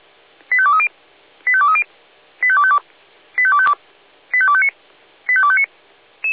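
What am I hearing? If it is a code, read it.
Five-tone selective call (ZVEI), heard as quick runs of five tones, tones of about 70 ms. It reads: 85230 85230 85252 85252 85259 85259
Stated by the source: ZVEI five-tone selective call received on an SDR.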